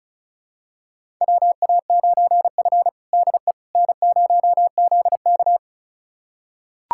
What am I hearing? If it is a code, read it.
Morse code, heard as a keyed tone, about 36 wpm, 700 Hz. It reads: WA9F DE N0ZK
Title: USA Callsign DE Callsign - 4 Characters - 35wpm